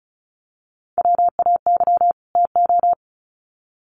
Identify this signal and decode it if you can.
Morse code, heard as a keyed tone, about 34 wpm, 700 Hz. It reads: WAY TO
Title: Sets of 2 Words 35wpm